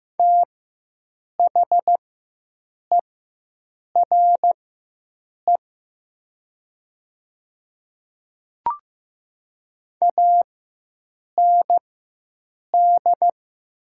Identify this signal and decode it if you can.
Morse code, heard as a keyed tone, about 15 wpm, spaced as for 7 wpm, 700 Hz. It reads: THERE AND
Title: 4x Character Spacing - Top 100 Words - 15wpm